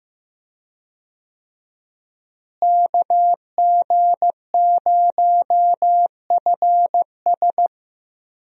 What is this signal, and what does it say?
Morse code, heard as a keyed tone, about 15 wpm, 700 Hz. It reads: KG0FS
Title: USA Call Signs - 5 Characters - Repeated 2x - 15wpm